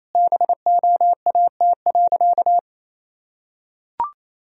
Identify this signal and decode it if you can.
Morse code, heard as a keyed tone, about 27 wpm, 700 Hz. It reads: BOAT.
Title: Sentences from Top 1000 Words - Review Part 3 - 28wpm